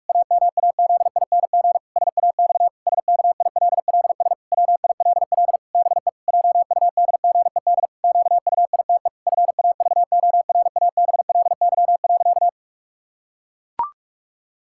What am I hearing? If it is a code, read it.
Morse code, heard as a keyed tone, about 41 wpm, 700 Hz. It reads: AMAZING SAX SKILLS WILL BE JUDGED QUITE FAVORABLY.
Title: Pangram Challenge v2 - 45wpm